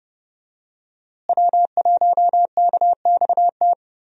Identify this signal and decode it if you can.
Morse code, heard as a keyed tone, about 31 wpm, 700 Hz. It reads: W1KXT
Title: USA Call Signs - 5 Characters - Repeated 2x - 30wpm